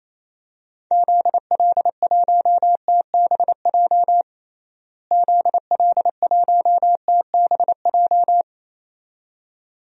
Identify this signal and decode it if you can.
Morse code, heard as a keyed tone, about 28 wpm, 700 Hz. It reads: ZL1TBJ ZL1TBJ
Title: DXCC-100 Call Signs - 6 Characters - Repeated 2x - 28wpm